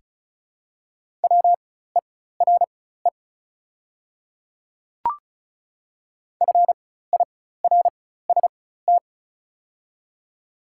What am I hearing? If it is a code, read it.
Morse code, heard as a keyed tone, about 34 wpm, spaced as for 16 wpm, 690 Hz. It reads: WERE FIRST
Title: 4x Character Spacing - Top 500 Words - 35wpm